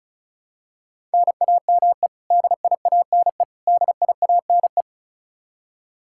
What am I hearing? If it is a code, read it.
Morse code, heard as a keyed tone, about 36 wpm, 700 Hz. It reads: NAME DIANE DIANE
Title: Practice QSO - Exchange 1 - Beginning - 35wpm